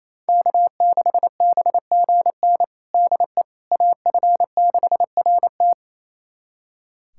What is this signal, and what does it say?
Morse code, heard as a keyed tone, about 28 wpm, 700 Hz. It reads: K6BGN DE AF6RT
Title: USA Callsign DE Callsign - 5 Characters - 28wpm